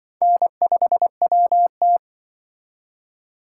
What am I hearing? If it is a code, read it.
Morse code, heard as a keyed tone, about 25 wpm, 700 Hz. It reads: N5WT
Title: Speed Racing - USA 1x2 Call Signs 20wpm